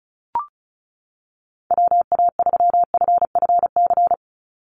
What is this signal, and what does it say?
Morse code, heard as a keyed tone, about 34 wpm, 700 Hz. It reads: WA3FFC
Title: USA Call Signs 35wpm